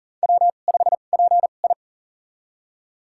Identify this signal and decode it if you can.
Morse code, heard as a keyed tone, about 40 wpm, spaced as for 29 wpm, 700 Hz. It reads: W5PI
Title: USA Call Signs - 2x Character Spacing - 40wpm